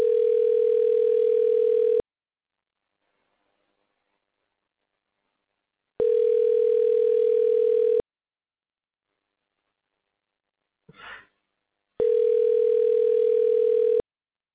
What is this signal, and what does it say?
Ringback tone (North America), 440 + 480 Hz.